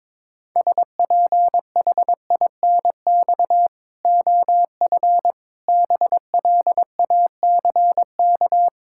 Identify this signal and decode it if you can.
Morse code, heard as a keyed tone, about 21 wpm, 700 Hz. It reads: SPHINX OF BLACK
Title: Pangram Challenge v2 - 22wpm